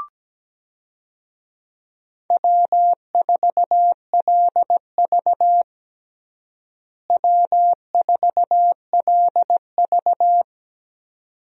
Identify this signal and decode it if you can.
Morse code, heard as a keyed tone, about 17 wpm, 700 Hz. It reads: W4LV W4LV
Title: USA Call Signs - 4 Characters - Repeated 2x - 17wpm